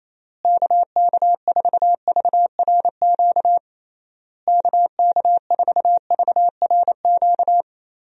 Morse code, heard as a keyed tone, about 28 wpm, 700 Hz. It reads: KK4VRQ KK4VRQ
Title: USA Call Signs - 6 Characters - Repeated 3x - 28wpm